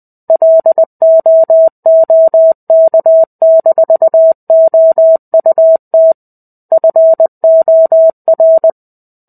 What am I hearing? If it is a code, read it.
Morse code, heard as a keyed tone, about 20 wpm, 650 Hz. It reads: LOOK-OUT FOR
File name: 05-Chapter-05 20 WPM